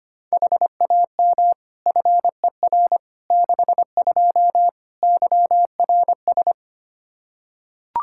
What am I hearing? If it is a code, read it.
Morse code, heard as a keyed tone, about 25 wpm, 700 Hz. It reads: HAM FER 62 YRS
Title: K1USN SST QSO Practice 25wpm